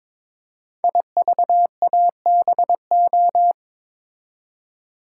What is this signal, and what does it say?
Morse code, heard as a keyed tone, about 23 wpm, 700 Hz. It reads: IVABO
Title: Code Groups - 5 Letters - 22wpm